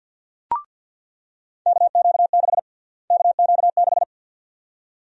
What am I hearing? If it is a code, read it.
Morse code, heard as a keyed tone, about 49 wpm, 700 Hz. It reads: KQ6 KQ6
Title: USA Call Sign Prefixes - Repeated 2x - 50wpm